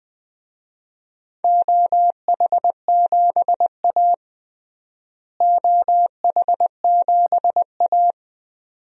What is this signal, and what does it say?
Morse code, heard as a keyed tone, about 21 wpm, 700 Hz. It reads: OH7A OH7A
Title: DXCC-100 Call Signs - 4 Characters - Repeated 2x - 20wpm